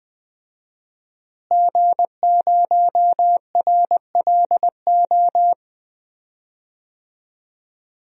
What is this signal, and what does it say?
Morse code, heard as a keyed tone, about 21 wpm, 700 Hz. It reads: G0RLO
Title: DXCC-100 Call Signs - 5 Characters - Repeated 3x - 20wpm